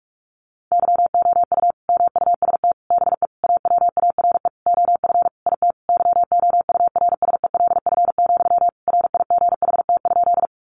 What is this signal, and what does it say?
Morse code, heard as a keyed tone, about 44 wpm, 700 Hz. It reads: YOU MUST BE AWARE OF IT YOURSELF, RIGHT?
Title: Sentences from Top 1200 Words 45wpm